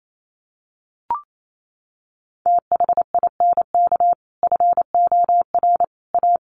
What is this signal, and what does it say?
Morse code, heard as a keyed tone, about 28 wpm, 700 Hz. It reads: THINK FOR A
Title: Sentences from Top 1000 Words 28wpm